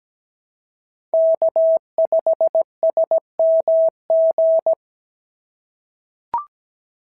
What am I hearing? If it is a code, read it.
Morse code, heard as a keyed tone, about 17 wpm, 690 Hz. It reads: K5SMG
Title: SOTA QSO Practice 17wpm